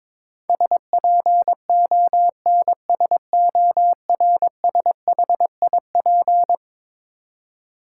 Morse code, heard as a keyed tone, about 22 wpm, 700 Hz. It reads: SPONSORSHIP
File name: practice / 22wpm / Eleven Letter Words - 22wpm